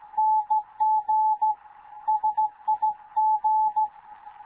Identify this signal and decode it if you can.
Morse code, heard as a keyed tone, about 15 wpm, 860 Hz. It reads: NG SIG
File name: Morze-HA2MN / hangp / 2 11 GYAKORLAT